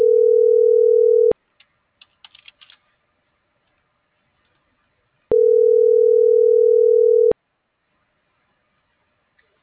Ringback tone (North America), 440 + 480 Hz.